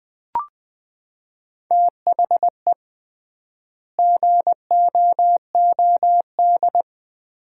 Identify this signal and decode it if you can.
Morse code, heard as a keyed tone, about 20 wpm, 700 Hz. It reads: THE GOOD
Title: 3x Word-Spacing - Sets of 5 Words - 20wpm